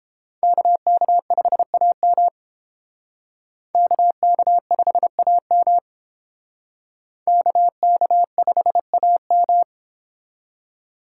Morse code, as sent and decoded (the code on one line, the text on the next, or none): -.- -.- ..... .- -- / -.- -.- ..... .- -- / -.- -.- ..... .- --
KK5AM KK5AM KK5AM